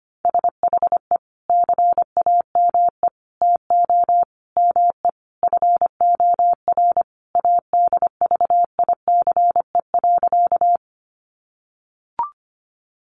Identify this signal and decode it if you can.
Morse code, heard as a keyed tone, about 25 wpm, 700 Hz. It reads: SHE CAME TO ME FOR ADVICE.
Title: Sentences from Top 1100 Words 25wpm